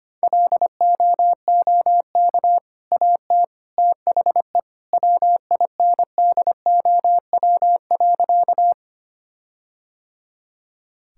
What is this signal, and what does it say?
Morse code, heard as a keyed tone, about 25 wpm, 700 Hz. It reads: LOOK AT THE WINDOW.
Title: Sentences from Top 1600 Words 25wpm